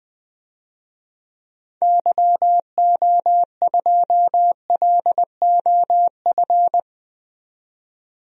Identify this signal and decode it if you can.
Morse code, heard as a keyed tone, about 20 wpm, 700 Hz. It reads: YO2LOF
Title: DXCC-100 Call Signs - 6 Characters - 20wpm